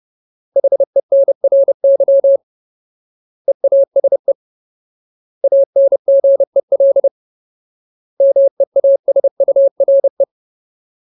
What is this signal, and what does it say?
Morse code, heard as a keyed tone, about 30 wpm, 550 Hz. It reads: HENRY EASE ANGEL MEASURE
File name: Ultimate Word List / 30WPM Ultimate